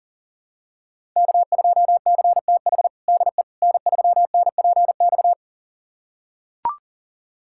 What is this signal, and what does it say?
Morse code, heard as a keyed tone, about 40 wpm, 700 Hz. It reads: K2CTH DE N3NPX
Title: USA Callsign DE Callsign - 5 Characters - 40wpm